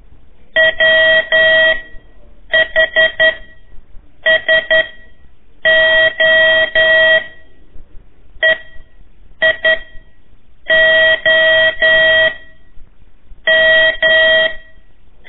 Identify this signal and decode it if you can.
Morse code, heard as a keyed tone, about 11 wpm, 1900 Hz. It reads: W H S O E I O M
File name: 3 lezione CW 1 AIR